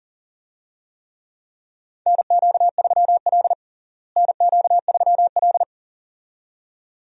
Morse code, heard as a keyed tone, about 37 wpm, 700 Hz. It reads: NQ3L NQ3L
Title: USA Call Signs - 4 Characters - Repeated 2x - 40wpm